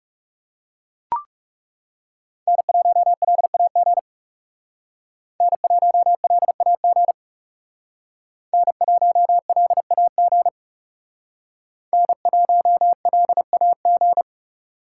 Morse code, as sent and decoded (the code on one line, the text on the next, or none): -. .---- .-.. .- --. / -. .---- .-.. .- --. / -. .---- .-.. .- --. / -. .---- .-.. .- --.
N1LAG N1LAG N1LAG N1LAG